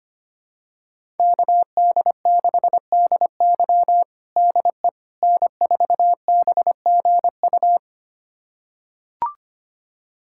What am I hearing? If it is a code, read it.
Morse code, heard as a keyed tone, about 25 wpm, 700 Hz. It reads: KD6DY DE N4BGU
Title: USA Callsign DE Callsign - 5 Characters - 25wpm